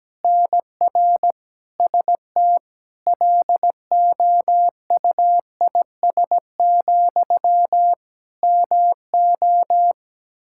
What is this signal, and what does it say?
Morse code, heard as a keyed tone, about 17 wpm, 700 Hz. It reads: NR ST LOUIS, MO
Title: Practice QSO - Exchange 1 - Beginning - 17wpm